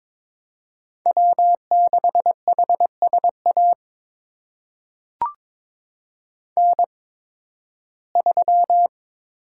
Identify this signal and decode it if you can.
Morse code, heard as a keyed tone, about 22 wpm, spaced as for 14 wpm, 700 Hz. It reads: W6HSA N3
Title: USA Call Signs - 5 Characters - 8x Character Spacing - 22wpm